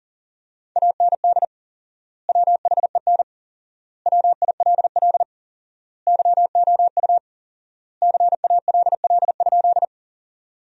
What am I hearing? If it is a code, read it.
Morse code, heard as a keyed tone, about 40 wpm, 700 Hz. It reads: AND WHEN WILL YOU CALL?
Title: 4x Word-Spacing - Sentences from Top 100 Words - 40wpm